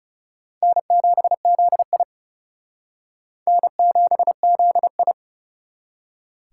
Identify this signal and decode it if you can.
Morse code, heard as a keyed tone, about 37 wpm, 700 Hz. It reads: N7ZI N7ZI
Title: Speed Racing - USA 1x2 Call Signs 30wpm